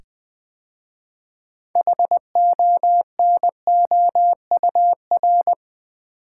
Morse code, heard as a keyed tone, about 20 wpm, 700 Hz. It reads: HONOUR
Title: 6 Letter Words - First 3 Letters Given - 20wpm